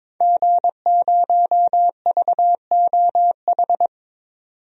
Morse code, heard as a keyed tone, about 22 wpm, 700 Hz. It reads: G0VOH
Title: DXCC-100 Call Signs - 5 Characters - 22wpm